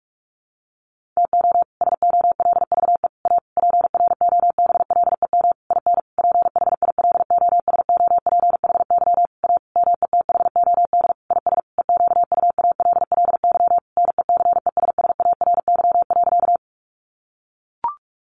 Morse code, heard as a keyed tone, about 44 wpm, 700 Hz. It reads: TO SOLVE A PROBLEM IN PHILOSOPHY A METHOD IS EQUALLY NECESSARY.
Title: Sentences from Top 1400 Words 45wpm